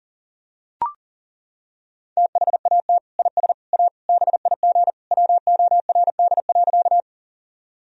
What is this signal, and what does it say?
Morse code, heard as a keyed tone, about 40 wpm, 700 Hz. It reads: THAT IS A BIG WORD.
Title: Sentences from Top 1000 Words - Review Part 1 - 40wpm